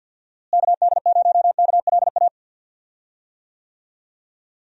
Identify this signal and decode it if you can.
Morse code, heard as a keyed tone, about 49 wpm, 700 Hz. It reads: KD0KLA
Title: USA Call Signs - 6 Characters - Repeated 3x - 50wpm